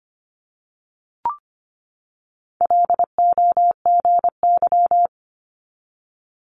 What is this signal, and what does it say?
Morse code, heard as a keyed tone, about 25 wpm, 700 Hz. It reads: LOGY
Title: Multi-letter Phonemes 25wpm